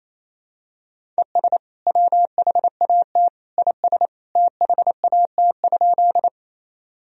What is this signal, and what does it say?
Morse code, heard as a keyed tone, about 28 wpm, 700 Hz. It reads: ES WHAT IS THAT?
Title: Sentences from Top 100 Words and Abbreviations 28wpm